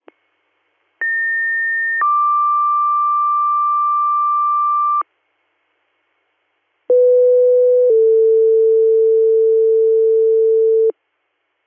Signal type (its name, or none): two-tone page